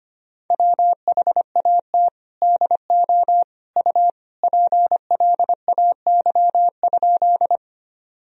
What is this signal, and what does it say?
Morse code, heard as a keyed tone, about 25 wpm, 700 Hz. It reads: WHAT DO U PLAY?
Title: Sentences from Top 200 Words and Abbreviations 25wpm